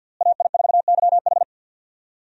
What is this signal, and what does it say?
Morse code, heard as a keyed tone, about 50 wpm, 700 Hz. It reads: AI4YH
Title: USA Call Signs - 5 Characters - Repeated 2x - 50wpm